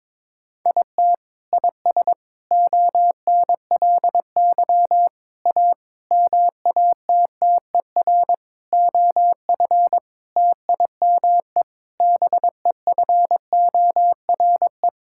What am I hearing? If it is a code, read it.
Morse code, heard as a keyed tone, about 22 wpm, 700 Hz. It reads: IT IS ONLY A MATTER OF TIME BEFORE
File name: Sets of 9 Words - 22wpm